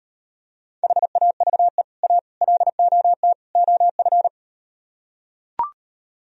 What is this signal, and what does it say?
Morse code, heard as a keyed tone, about 38 wpm, 700 Hz. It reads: HAVE A LOT OF